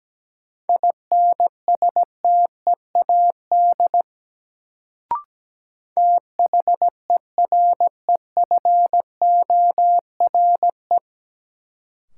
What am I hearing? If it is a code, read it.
Morse code, heard as a keyed tone, about 17 wpm, 700 Hz. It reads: INSTEAD THEREFORE
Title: Sentences from Top 800 Words - New Word Drill - 17wpm